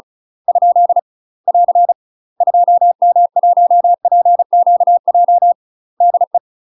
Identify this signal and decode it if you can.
Morse code, heard as a keyed tone, about 36 wpm, 700 Hz. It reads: ? <AR> 2M1PQJ DE
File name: UK MASTER QSOs TRAINING / 35WPM / UK MASTER QSOs 0010